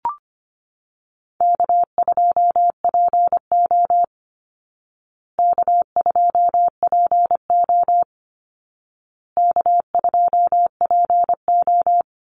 Morse code, heard as a keyed tone, about 25 wpm, 700 Hz. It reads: K2PO K2PO K2PO
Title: Call Signs Repeated 2x - 25wpm